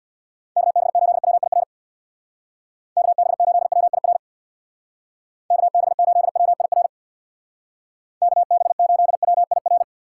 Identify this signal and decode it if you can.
Morse code, heard as a keyed tone, about 64 wpm, 700 Hz. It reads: KB8WIR KB8WIR KB8WIR KB8WIR